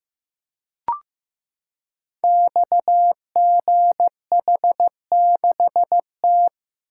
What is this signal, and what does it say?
Morse code, heard as a keyed tone, about 15 wpm, 700 Hz. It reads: XGH6T
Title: Code Groups - 5 Letter:Numbers - 15wpm